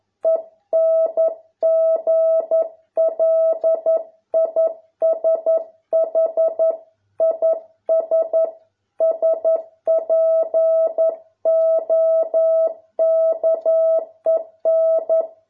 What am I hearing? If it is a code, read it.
Morse code, heard as a keyed tone, about 11 wpm, 650 Hz. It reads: ENGLISH IS SPOKEN